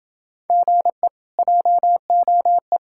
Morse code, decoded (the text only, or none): GE JOE